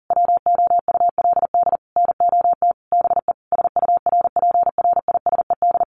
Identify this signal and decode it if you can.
Morse code, heard as a keyed tone, about 40 wpm, 700 Hz. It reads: WOULD NOT BE SURPRISED
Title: Sets of 5 Words Encore2 40wpm